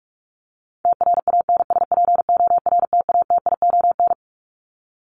Morse code, heard as a keyed tone, about 44 wpm, 700 Hz. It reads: TRANSPORTATION
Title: Top 1501-1750 Words 45wpm